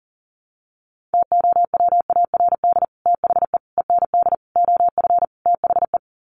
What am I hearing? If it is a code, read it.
Morse code, heard as a keyed tone, about 37 wpm, 700 Hz. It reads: TOWARD THE END OF THE
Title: Sets of 5 Words 40wpm